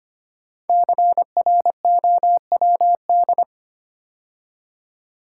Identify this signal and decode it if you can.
Morse code, heard as a keyed tone, about 25 wpm, 700 Hz. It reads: CROWD
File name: 5 Letter Words - First 3 Letters Given - 25wpm